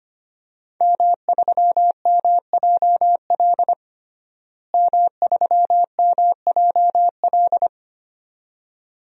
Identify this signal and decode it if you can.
Morse code, heard as a keyed tone, about 25 wpm, 700 Hz. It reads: M3MJL M3MJL
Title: DXCC-100 Call Signs - 5 Characters - Repeated 2x - 25wpm